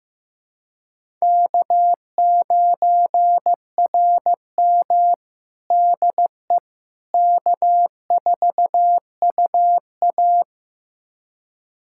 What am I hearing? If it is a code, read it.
Morse code, heard as a keyed tone, about 15 wpm, 700 Hz. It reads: K9RM DE K4UA
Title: USA Callsign DE Callsign - 4 Characters - 15wpm